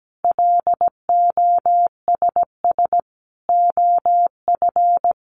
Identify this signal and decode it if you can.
Morse code, heard as a keyed tone, about 17 wpm, 700 Hz. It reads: LOSS OF
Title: Sentences from Top 1400 Words 17wpm